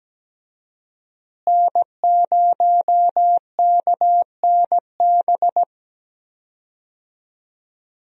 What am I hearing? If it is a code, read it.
Morse code, heard as a keyed tone, about 17 wpm, 700 Hz. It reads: N0KNB
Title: USA Call Signs - 5 Characters - Repeated 2x - 17wpm